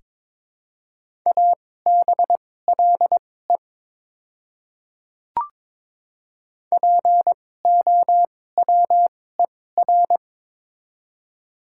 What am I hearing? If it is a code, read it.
Morse code, heard as a keyed tone, about 22 wpm, spaced as for 16 wpm, 690 Hz. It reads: ABLE POWER
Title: 2x Character Spacing - Top 500 Words - 22wpm